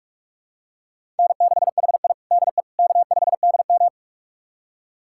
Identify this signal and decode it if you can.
Morse code, heard as a keyed tone, about 46 wpm, 700 Hz. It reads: N6HI DE K5DM
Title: USA Callsign DE Callsign - 4 Characters - 45wpm